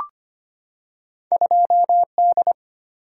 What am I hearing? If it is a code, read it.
Morse code, heard as a keyed tone, about 25 wpm, 700 Hz. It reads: 2D
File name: Two Letter-Number Combinations - 25wpm